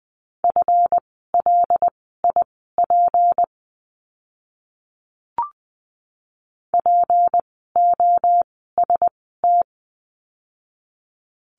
Morse code, heard as a keyed tone, about 20 wpm, spaced as for 14 wpm, 700 Hz. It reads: FLIP POST
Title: 153 Lesson 25f - Letter p - Words 20wpm